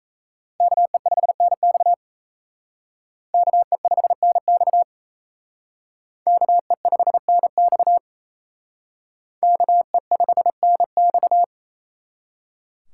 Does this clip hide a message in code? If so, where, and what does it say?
Yes: Morse code from 0.6 to 11.4 s, heard as a keyed tone, about 37 wpm, 700 Hz. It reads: KE5NX KE5NX KE5NX KE5NX